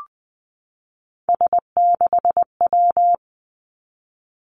Morse code, heard as a keyed tone, about 20 wpm, 700 Hz. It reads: S6W
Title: Three Challenging Letter-Number Combinations 20wpm